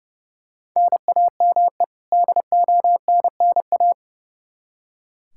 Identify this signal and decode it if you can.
Morse code, heard as a keyed tone, about 30 wpm, 700 Hz. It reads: NAME DONNA
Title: Practice QSO - Exchange 1 - Beginning - 30wpm